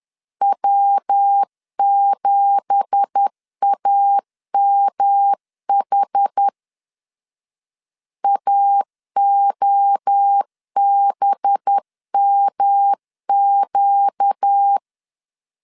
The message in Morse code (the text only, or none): W7AMH AOBMQ